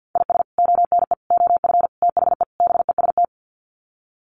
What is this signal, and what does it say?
Morse code, heard as a keyed tone, about 50 wpm, 700 Hz. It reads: IS ONE OF THE BEST
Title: Sets of 5 Words 50wpm